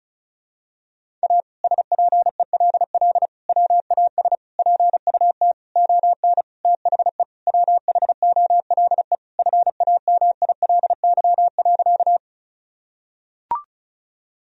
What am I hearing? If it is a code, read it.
Morse code, heard as a keyed tone, about 34 wpm, 700 Hz. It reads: A SPELL WAS PUT ON THE WHOLE FAMILY.
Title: Sentences from Top 1000 Words - Review Part 1 - 35wpm